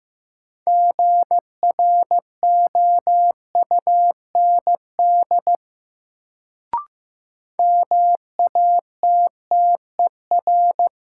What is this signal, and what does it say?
Morse code, heard as a keyed tone, about 16 wpm, 700 Hz. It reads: GROUND MATTER
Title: Sets of 3 Words - New Word Drill - 15wpm